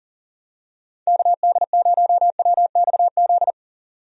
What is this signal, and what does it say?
Morse code, heard as a keyed tone, about 38 wpm, 700 Hz. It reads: KD0WXZ
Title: USA Call Signs - 6 Characters - 40wpm